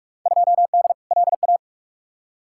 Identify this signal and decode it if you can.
Morse code, heard as a keyed tone, about 46 wpm, 700 Hz. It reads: 2D LA
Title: Field Day Exchanges - 45wpm